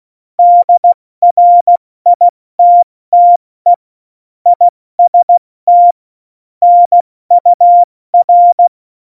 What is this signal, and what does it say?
Morse code, heard as a keyed tone, about 16 wpm, 700 Hz. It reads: DRITTE IST NUR
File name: Morsekurs / Witze / Witz 29. Dezember 2020, 70 F80